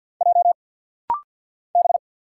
Morse code, heard as a keyed tone, about 50 wpm, 700 Hz. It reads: J B
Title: Single Letters - Rapid-Fire 50wpm